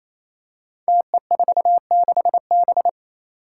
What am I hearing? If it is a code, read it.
Morse code, heard as a keyed tone, about 28 wpm, 700 Hz. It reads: TE46B
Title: Five Challenging Letter-Number Combinations 28wpm